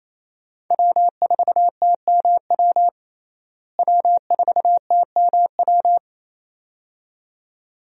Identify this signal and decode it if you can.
Morse code, heard as a keyed tone, about 28 wpm, 700 Hz. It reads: W4TMW W4TMW
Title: USA Call Signs - 5 Characters - Repeated 2x - 28wpm